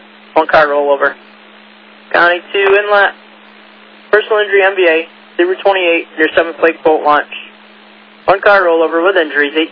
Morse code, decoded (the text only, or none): none